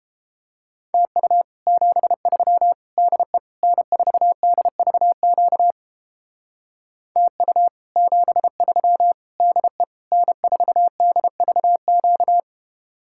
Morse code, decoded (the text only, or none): TU 73 DE N4DVQ TU 73 DE N4DVQ